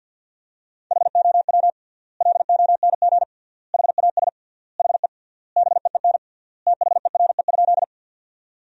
Morse code, heard as a keyed tone, about 50 wpm, 700 Hz. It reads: HOW LONG HAS HE BEEN THERE?
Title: 3x Word-Spacing - Sentences from Top 100 Words - 50wpm